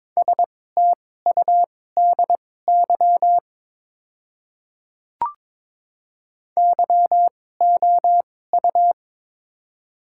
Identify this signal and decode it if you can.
Morse code, heard as a keyed tone, about 22 wpm, spaced as for 16 wpm, 700 Hz. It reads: STUDY YOU
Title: 2x Character Spacing - Top 300 Words - 22wpm